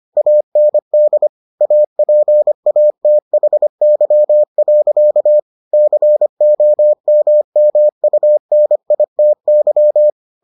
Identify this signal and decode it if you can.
Morse code, heard as a keyed tone, about 25 wpm, 600 Hz. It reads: AND APATHY. COMMUNITY